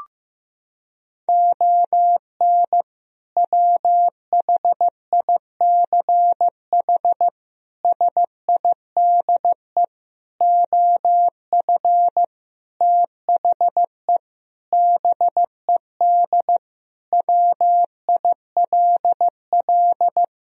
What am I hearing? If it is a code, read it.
Morse code, heard as a keyed tone, about 15 wpm, 700 Hz. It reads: ON WHICH SIDE OF THE BED WILL
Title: Sentences from Top 1000 Words - Review Part 1 - 15wpm